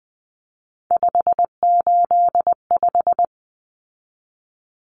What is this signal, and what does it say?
Morse code, heard as a keyed tone, about 20 wpm, 700 Hz. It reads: 585
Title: Common Cut Numbers - 3-Digit 20wpm